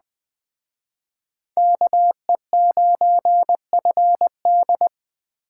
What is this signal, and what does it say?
Morse code, heard as a keyed tone, about 21 wpm, 700 Hz. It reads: KE9FD
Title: Speed Racing - USA 2x2 Call Signs 17wpm